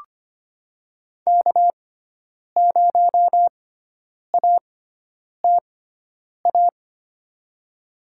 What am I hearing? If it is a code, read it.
Morse code, heard as a keyed tone, about 25 wpm, spaced as for 9 wpm, 700 Hz. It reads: K0ATA
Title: USA Call Signs - 5 Characters - 6x Character Spacing - 25wpm